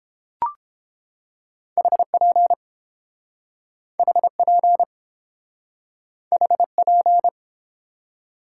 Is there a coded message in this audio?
Yes: Morse code from 1.8 to 7.3 s, heard as a keyed tone, about 30 wpm, 700 Hz. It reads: HP HP HP